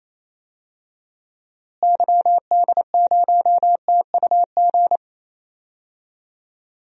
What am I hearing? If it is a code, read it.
Morse code, heard as a keyed tone, about 28 wpm, 700 Hz. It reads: YD0TUG
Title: DXCC-100 Call Signs - 6 Characters - Repeated 2x - 28wpm